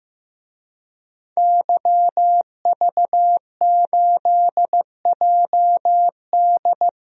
Morse code, heard as a keyed tone, about 15 wpm, 700 Hz. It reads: YV8JD
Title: DXCC-100 Call Signs - 5 Characters - 15wpm